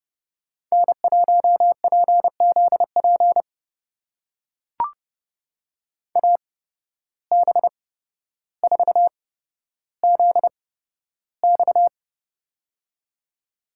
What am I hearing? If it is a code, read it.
Morse code, heard as a keyed tone, about 30 wpm, spaced as for 13 wpm, 700 Hz. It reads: N1PZP AB4ZX
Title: USA Call Signs - 5 Characters - 8x Character Spacing - 30wpm